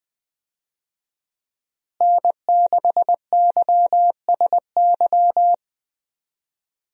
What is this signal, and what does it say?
Morse code, heard as a keyed tone, about 21 wpm, 700 Hz. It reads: N6YSY
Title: USA Call Signs - 5 Characters - Repeated 2x - 20wpm